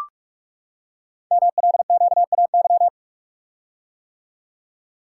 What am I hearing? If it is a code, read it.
Morse code, heard as a keyed tone, about 44 wpm, 690 Hz. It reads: MLQAY
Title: Code Groups - 5 Letters - 45wpm